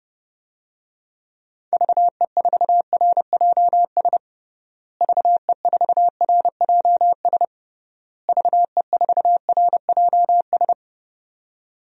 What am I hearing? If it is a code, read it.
Morse code, heard as a keyed tone, about 30 wpm, 700 Hz. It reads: VE4RJS VE4RJS VE4RJS
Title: DXCC-100 Call Signs - 6 Characters - Repeated 3x - 30wpm